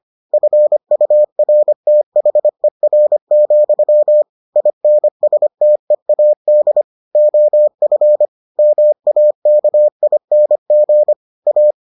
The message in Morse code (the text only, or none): FURTHER, INSTEAD OF MAKING A